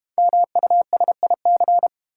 Morse code, heard as a keyed tone, about 32 wpm, 690 Hz. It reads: MUSIC